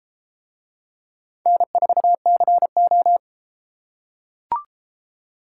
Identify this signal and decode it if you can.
Morse code, heard as a keyed tone, about 33 wpm, 700 Hz. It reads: N4CO